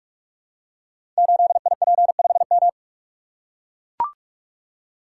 Morse code, heard as a keyed tone, about 44 wpm, 690 Hz. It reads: 8IP5M
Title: Code Groups - 5 Letter:Numbers - 45wpm